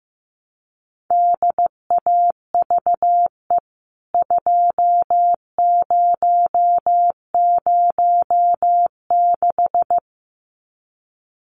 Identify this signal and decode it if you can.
Morse code, heard as a keyed tone, about 15 wpm, 700 Hz. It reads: DAVE 2006
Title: CWT Exchange 15wpm